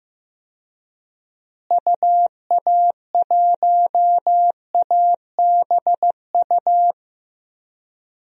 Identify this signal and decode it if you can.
Morse code, heard as a keyed tone, about 15 wpm, 700 Hz. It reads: UA1ABU
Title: DXCC-100 Call Signs - 6 Characters - Repeated 2x - 15wpm